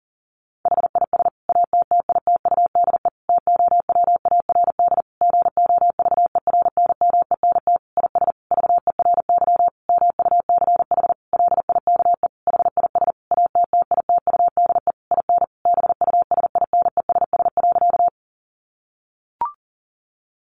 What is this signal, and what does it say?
Morse code, heard as a keyed tone, about 40 wpm, 700 Hz. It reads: HIS ATTITUDE TOWARD GOVERNMENT IS VERY MUCH LIKE HIS ATTITUDE IN BUSINESS.
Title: Sentences from Top 1300 Words 40wpm